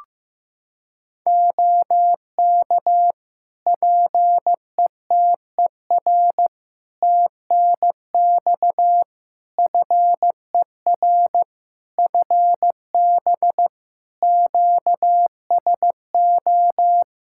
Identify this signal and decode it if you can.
Morse code, heard as a keyed tone, about 16 wpm, 700 Hz. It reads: OK PETER TNX FER FB QSO
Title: Practice QSO - Exchange 4 - Ending - 15wpm